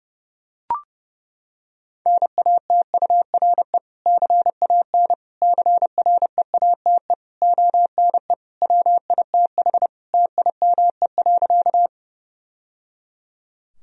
Morse code, heard as a keyed tone, about 32 wpm, 700 Hz. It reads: NATURE CAN CREATE ONE WITH TIME.
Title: Sentences from Top 1000 Words - Review Part 3 - 30wpm